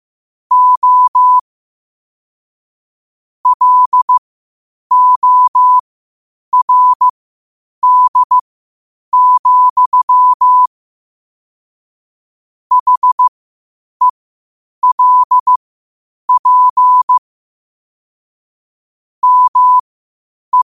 Morse code, heard as a keyed tone, about 15 wpm, spaced as for 8 wpm, 1000 Hz. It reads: O LORD, HELP ME
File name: qotdfiles / 8wpm